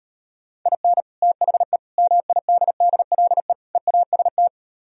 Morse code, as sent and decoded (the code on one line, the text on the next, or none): .. -. / - .... / -- .. -.. -.. .-.. / . .- ... -
IN THE MIDDLE EAST